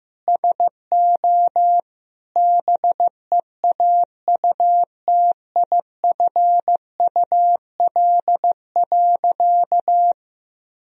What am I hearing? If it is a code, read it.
Morse code, heard as a keyed tone, about 16 wpm, 700 Hz. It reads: SO BEAUTIFUL.
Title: Sentences from Top 1000 Words - Review Part 1 - 15wpm